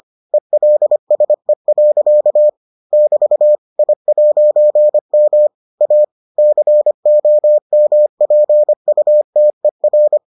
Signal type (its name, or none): Morse code